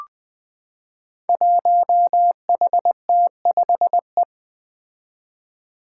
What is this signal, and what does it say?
Morse code, heard as a keyed tone, about 20 wpm, 700 Hz. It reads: 1HT5E
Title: Five Challenging Letter-Number Combinations 20wpm